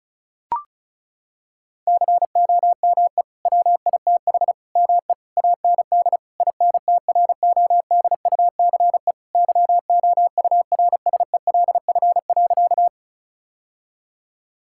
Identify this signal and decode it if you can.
Morse code, heard as a keyed tone, about 36 wpm, 690 Hz. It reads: COME WITH ME AND INTRODUCE YOURSELF.
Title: Sentences from Top 1600 Words 35wpm